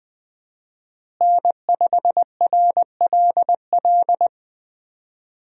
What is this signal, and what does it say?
Morse code, heard as a keyed tone, about 20 wpm, 700 Hz. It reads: N5RLL
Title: USA Call Signs - 5 Characters - Repeated 2x - 20wpm